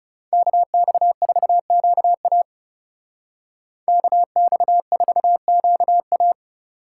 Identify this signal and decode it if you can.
Morse code, heard as a keyed tone, about 35 wpm, 700 Hz. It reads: KX4QA KX4QA
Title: Speed Racing - USA 2x2 Call Signs 30wpm